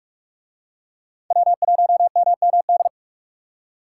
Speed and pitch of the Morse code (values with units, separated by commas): 44 wpm, 700 Hz